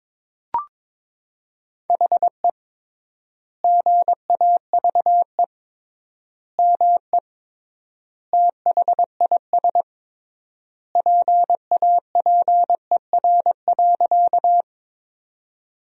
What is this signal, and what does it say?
Morse code, heard as a keyed tone, about 22 wpm, 700 Hz. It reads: HE GAVE ME THIS PAPER.
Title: 3x Word-Spacing - Sentences from Top 500 Words - 22wpm